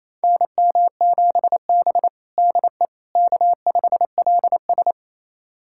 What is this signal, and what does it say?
Morse code, heard as a keyed tone, about 27 wpm, 700 Hz. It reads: NM7B DE K5LS
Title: USA Callsign DE Callsign - 4 Characters - 28wpm